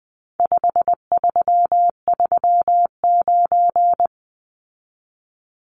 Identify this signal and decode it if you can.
Morse code, heard as a keyed tone, about 20 wpm, 700 Hz. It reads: 5339
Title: Numbers - 4-Digit 20wpm